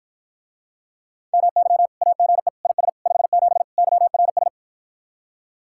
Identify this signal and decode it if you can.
Morse code, heard as a keyed tone, about 53 wpm, 700 Hz. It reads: MY AGE IS 57 YRS